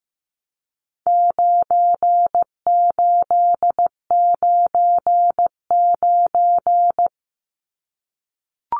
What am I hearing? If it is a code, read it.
Morse code, heard as a keyed tone, about 15 wpm, 700 Hz. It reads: 9899
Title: Numbers - 4-Digit 15wpm